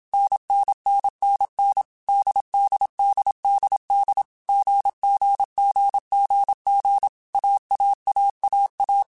Morse code, heard as a keyed tone, about 27 wpm, 800 Hz. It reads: NNNNN DDDDD GGGGG AAAAA
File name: Basic Sets / Basic Letter Groups Collective7 / AWUNDG@25